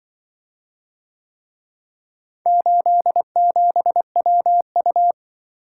Morse code, heard as a keyed tone, about 24 wpm, 700 Hz. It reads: 87WU
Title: Speed-Racing - Four Letter-Number Combinations - 20wpm